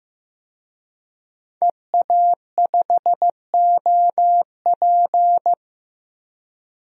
Morse code, heard as a keyed tone, about 16 wpm, 700 Hz. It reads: EA5OP
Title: DXCC-100 Call Signs - 5 Characters - Repeated 2x - 15wpm